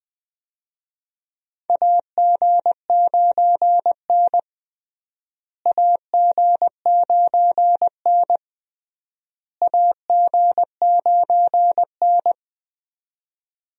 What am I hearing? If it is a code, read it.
Morse code, heard as a keyed tone, about 21 wpm, 700 Hz. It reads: AG9N AG9N AG9N
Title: USA Call Signs - 4 Characters - Repeated 3x - 20wpm